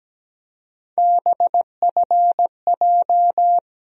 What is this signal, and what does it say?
Morse code, heard as a keyed tone, about 17 wpm, 700 Hz. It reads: BFJ
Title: USA Call Sign Postfixes - 3 Letters 17wpm